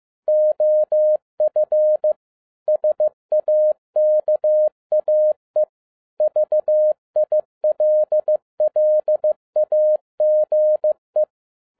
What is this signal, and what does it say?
Morse code, heard as a keyed tone, about 15 wpm, 610 Hz. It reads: OF SAKAE VILLAGE
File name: 2011-10-mp3 / n02-FIELD DAY CONTEST QRV FROM SAKAE VILLAGE 15wpm